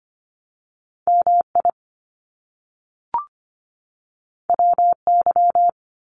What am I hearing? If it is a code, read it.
Morse code, heard as a keyed tone, about 25 wpm, 700 Hz. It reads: MI WY
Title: US States 25wpm